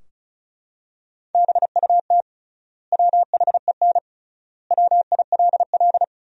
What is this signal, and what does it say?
Morse code, heard as a keyed tone, about 36 wpm, 700 Hz. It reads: BUT WHEN WILL
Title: 3x Word-Spacing - Sentences from Top 100 Words - 35wpm